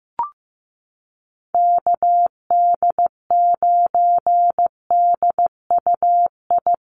Morse code, heard as a keyed tone, about 15 wpm, 700 Hz. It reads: KD9DUI
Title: USA Call Signs Repeated 3x 15wpm